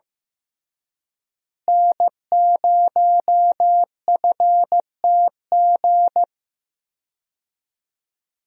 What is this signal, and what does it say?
Morse code, heard as a keyed tone, about 15 wpm, 700 Hz. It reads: N0FTG
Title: USA Call Signs - 5 Characters - Repeated 2x - 15wpm